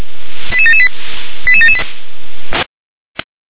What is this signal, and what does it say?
Five-tone selective call (ZVEI), heard as quick runs of five tones, tones of about 70 ms. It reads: 88778 88778